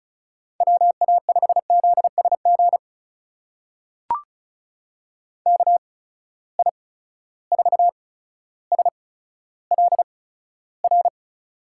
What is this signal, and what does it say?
Morse code, heard as a keyed tone, about 36 wpm, spaced as for 15 wpm, 700 Hz. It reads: WA5ZSG KI4SLR
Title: USA Call Signs - 6 Characters - 8x Character Spacing - 35wpm